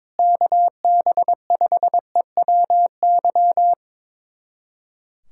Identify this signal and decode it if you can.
Morse code, heard as a keyed tone, about 23 wpm, 700 Hz. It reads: KB5EWY
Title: USA Callsign DE Callsign - 6 Characters - 22wpm